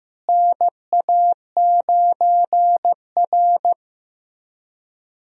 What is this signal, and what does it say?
Morse code, heard as a keyed tone, about 15 wpm, 700 Hz. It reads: NA9R
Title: USA Call Signs - 4 Characters - Repeated 2x - 15wpm